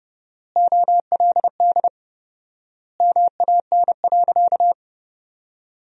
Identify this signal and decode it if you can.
Morse code, heard as a keyed tone, about 30 wpm, 700 Hz. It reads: OLD MAN.
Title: 4x Word-Spacing - Sentences from Top 200 Words - 30wpm